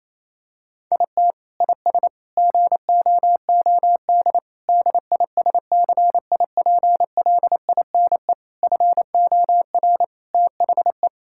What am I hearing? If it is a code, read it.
Morse code, heard as a keyed tone, about 28 wpm, 700 Hz. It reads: IT IS GOOD DISCIPLINE FOR THE
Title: Sentences from Top 1000 Words - Review Part 6 - 28wpm